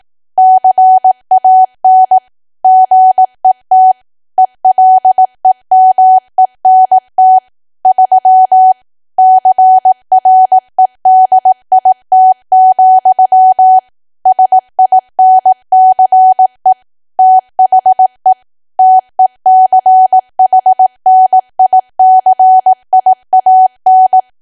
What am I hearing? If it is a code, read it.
Morse code, heard as a keyed tone, about 18 wpm, 750 Hz. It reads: CAN GET ELEMENT 3 CREDIT, SINCE THE TECHNICIAN
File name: cwa / W1AW-Level2 / 18WPM